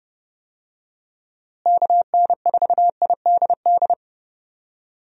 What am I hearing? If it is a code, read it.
Morse code, heard as a keyed tone, about 30 wpm, 700 Hz. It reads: KN4IDD
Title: USA Call Signs - 6 Characters - Repeated 3x - 30wpm